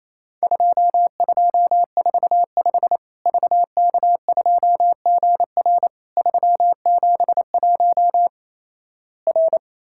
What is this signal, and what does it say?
Morse code, heard as a keyed tone, about 28 wpm, 700 Hz. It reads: 2245 VK2GR 371 R
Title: Worked All Europe DX Contest - QTC Exchange - 28wpm